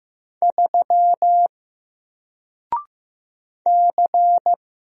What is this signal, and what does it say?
Morse code, heard as a keyed tone, about 15 wpm, 700 Hz. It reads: 3 C
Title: Single Character (Letter:Number:Punctuation:Prosign) Alt 15wpm